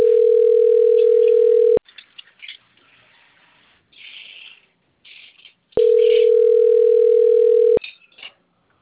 Ringback tone (North America), 440 + 480 Hz.
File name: rg-1600-unknown- - - .73763